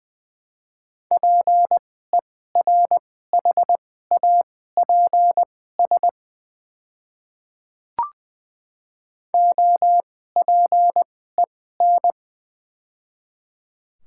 Morse code, heard as a keyed tone, about 21 wpm, spaced as for 15 wpm, 700 Hz. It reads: PERHAPS OPEN
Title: 2x Character Spacing - Top 500 Words - 20wpm